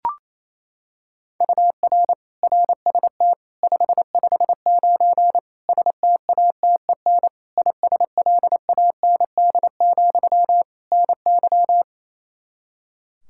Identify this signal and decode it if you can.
Morse code, heard as a keyed tone, about 28 wpm, 700 Hz. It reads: UR RST 559 STATEN ISLAND, NY
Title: Practice QSO - Exchange 1 - Beginning - 28wpm